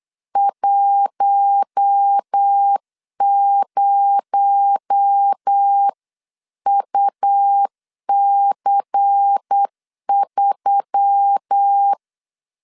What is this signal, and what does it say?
Morse code, heard as a keyed tone, about 8 wpm, 800 Hz. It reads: 10UC3
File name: LEZIONI / 7 GF9Z / 7 GF9Z 8wpm 3213